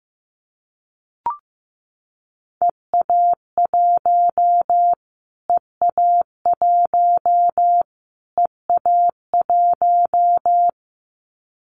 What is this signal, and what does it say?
Morse code, heard as a keyed tone, about 15 wpm, 700 Hz. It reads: EA1 EA1 EA1
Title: Callsign Prefixes for Top 25 DXCC - Repeated 3x - 15wpm